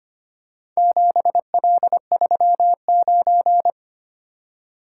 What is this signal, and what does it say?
Morse code, heard as a keyed tone, about 25 wpm, 700 Hz. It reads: 7L39
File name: practice / 25wpm / Four Letter-Number Combinations - 25wpm